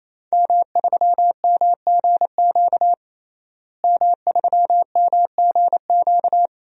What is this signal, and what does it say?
Morse code, heard as a keyed tone, about 28 wpm, 700 Hz. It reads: M3MGQ M3MGQ
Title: DXCC-100 Call Signs - 5 Characters - Repeated 3x - 28wpm